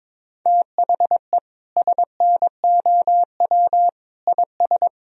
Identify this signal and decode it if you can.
Morse code, heard as a keyed tone, about 22 wpm, 700 Hz. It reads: THE SNOW IS
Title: Sentences from Top 1000 Words - Review Part 3 - 22wpm